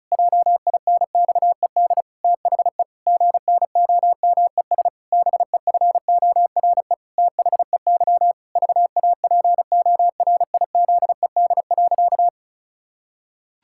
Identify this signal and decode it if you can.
Morse code, heard as a keyed tone, about 34 wpm, 700 Hz. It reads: JINXED THE GNOMES BEFORE THEY VAPORIZED.
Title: Pangram Challenge v2 - 35wpm